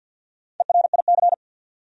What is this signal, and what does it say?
Morse code, heard as a keyed tone, about 51 wpm, 700 Hz. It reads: ERIC
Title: USA Common Names - Max 5 Letters - 50wpm